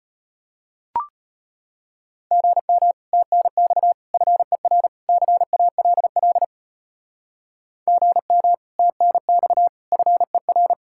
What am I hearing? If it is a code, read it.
Morse code, heard as a keyed tone, about 31 wpm, 700 Hz. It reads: GM TNX FER CALL GM TNX FER